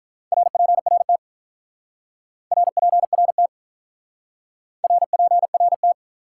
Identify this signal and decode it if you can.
Morse code, heard as a keyed tone, about 43 wpm, 700 Hz. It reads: RPRT RPRT RPRT
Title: Speed-Racing Top 180 QSO Elements 35wpm